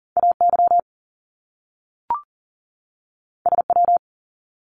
Morse code, heard as a keyed tone, about 40 wpm, 700 Hz. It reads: AY SW
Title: Top Two Letter Combinations in Words 40wpm